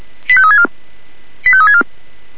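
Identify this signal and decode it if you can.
Five-tone selective call (ZVEI), heard as quick runs of five tones, tones of about 70 ms. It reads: 85364 85364
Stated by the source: Five-tone ZVEI selective call received on SDR.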